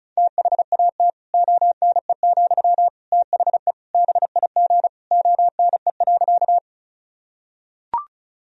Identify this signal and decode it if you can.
Morse code, heard as a keyed tone, about 36 wpm, 700 Hz. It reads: THAT ONE, THE BIG ONE.
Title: Sentences from Top 1000 Words - Review Part 1 - 35wpm